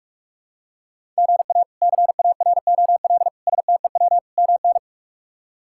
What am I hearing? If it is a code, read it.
Morse code, heard as a keyed tone, about 46 wpm, 700 Hz. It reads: GA CAROL STEW MN
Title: K1USN SST QSO Practice - 45wpm